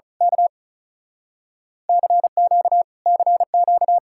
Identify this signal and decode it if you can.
Morse code, heard as a keyed tone, about 32 wpm, 700 Hz. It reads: K CQ CQ
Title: Speed-Racing Common QSO Phrases Encore - 35wpm